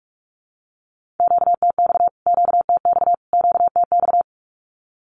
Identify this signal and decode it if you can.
Morse code, heard as a keyed tone, about 46 wpm, 700 Hz. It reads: QTX QTX QTX
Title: Q-Codes Repeated 3x 45wpm